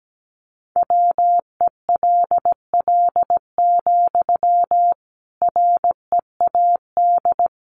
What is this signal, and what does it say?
Morse code, heard as a keyed tone, about 17 wpm, 700 Hz. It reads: WELL, READ
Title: Sentences from Top 200 Words 17wpm